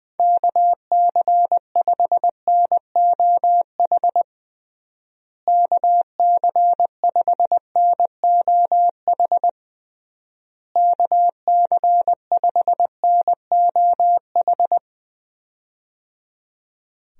Morse code, heard as a keyed tone, about 20 wpm, 700 Hz. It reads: KC5NOH KC5NOH KC5NOH
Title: USA Call Signs - 6 Characters - Repeated 3x - 20wpm